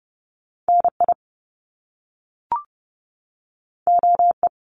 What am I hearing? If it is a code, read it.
Morse code, heard as a keyed tone, about 30 wpm, 700 Hz. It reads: N I O E
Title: Top Two Letter Combinations in Words 30wpm